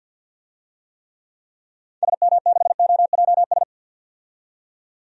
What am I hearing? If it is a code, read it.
Morse code, heard as a keyed tone, about 47 wpm, 700 Hz. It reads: SM6OJS
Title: DXCC-100 Call Signs - 6 Characters - 50wpm